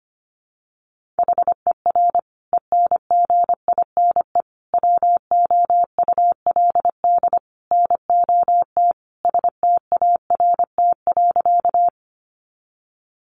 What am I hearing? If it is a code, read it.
Morse code, heard as a keyed tone, about 25 wpm, 700 Hz. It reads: HER ENGINE WOULD NOT START.
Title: Sentences from Top 1500 Words 25wpm